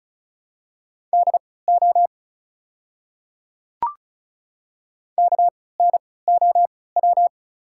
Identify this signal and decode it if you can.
Morse code, heard as a keyed tone, about 34 wpm, spaced as for 20 wpm, 690 Hz. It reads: DO KNOW
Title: 3x Character Spacing - Top 100 Words - 35wpm